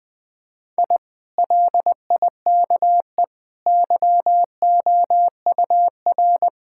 Morse code, heard as a keyed tone, about 20 wpm, 700 Hz. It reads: I LIKE YOUR
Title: Sentences from Top 1000 Words - Review Part 5 - 20wpm